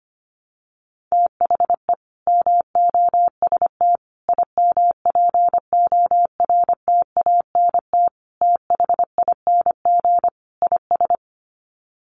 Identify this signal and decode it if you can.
Morse code, heard as a keyed tone, about 25 wpm, 700 Hz. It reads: THE MOST IMPORTANT THING IS
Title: Sets of 5 Words Encore 25wpm